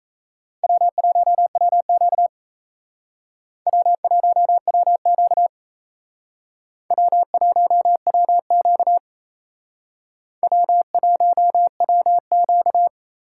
Morse code, heard as a keyed tone, about 36 wpm, 700 Hz. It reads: W1WQ W1WQ W1WQ W1WQ